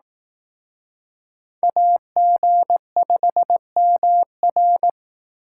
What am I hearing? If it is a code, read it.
Morse code, heard as a keyed tone, about 18 wpm, 700 Hz. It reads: AG5MR
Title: Speed Racing - USA 2x2 Call Signs 15wpm